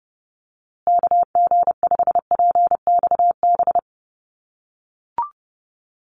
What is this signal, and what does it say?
Morse code, heard as a keyed tone, about 30 wpm, 700 Hz. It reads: KG5PXB
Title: USA Call Signs Repeated 2x 30wpm